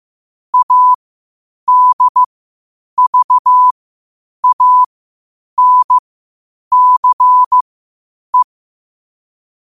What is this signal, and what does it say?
Morse code, heard as a keyed tone, about 15 wpm, spaced as for 8 wpm, 1000 Hz. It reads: ADVANCE
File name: qotdfiles / 8wpm